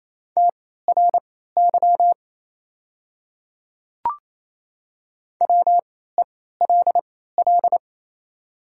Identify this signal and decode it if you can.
Morse code, heard as a keyed tone, about 28 wpm, spaced as for 16 wpm, 700 Hz. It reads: TRY WELL
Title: 3x Character Spacing - Top 200 Words - 28wpm